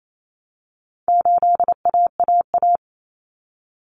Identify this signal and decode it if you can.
Morse code, heard as a keyed tone, about 28 wpm, 700 Hz. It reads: 8AAA
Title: Common Cut Numbers - 4-Digit 28wpm